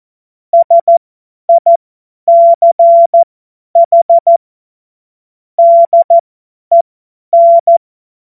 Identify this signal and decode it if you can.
Morse code, heard as a keyed tone, about 14 wpm, spaced as for 10 wpm, 680 Hz. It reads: SICH DEN
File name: Morsekurs / Witze / Witz 18. Januar 2022, 50 F70